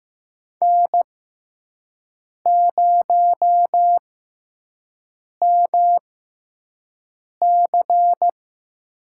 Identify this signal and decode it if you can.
Morse code, heard as a keyed tone, about 15 wpm, spaced as for 5 wpm, 700 Hz. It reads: N0MC
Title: USA Call Signs - 5 Characters - 6x Character Spacing - 15wpm